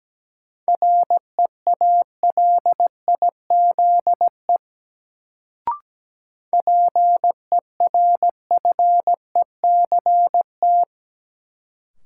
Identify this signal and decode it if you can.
Morse code, heard as a keyed tone, about 17 wpm, 700 Hz. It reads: REALIZE PERFECT
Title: Sentences from Top 1000 Words - New Word Drill - 17wpm